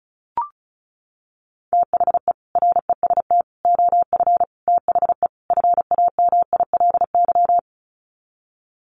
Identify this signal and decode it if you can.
Morse code, heard as a keyed tone, about 34 wpm, 700 Hz. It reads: THE REST OF THE FAMILY
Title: Sets of 5 Words Encore2 35wpm